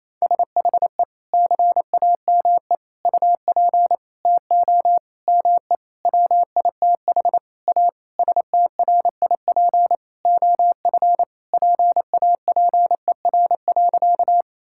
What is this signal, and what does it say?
Morse code, heard as a keyed tone, about 28 wpm, 700 Hz. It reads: SHE CAME UP TO ME WITH A STRIP OF PAPER.
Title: Sentences from Top 1600 Words 28wpm